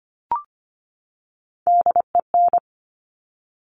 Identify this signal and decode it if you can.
Morse code, heard as a keyed tone, about 25 wpm, 700 Hz. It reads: DEN
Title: Top Three Letter Combinations in Words 25wpm